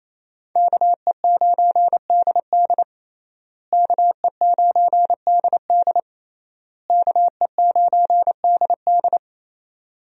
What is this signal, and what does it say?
Morse code, heard as a keyed tone, about 28 wpm, 700 Hz. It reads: KE9DD KE9DD KE9DD
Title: USA Call Signs - 5 Characters - Repeated 3x - 28wpm